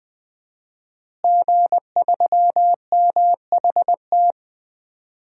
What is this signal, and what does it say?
Morse code, heard as a keyed tone, about 20 wpm, 700 Hz. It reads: G3MHT
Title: DXCC-100 Call Signs - 5 Characters - Repeated 3x - 20wpm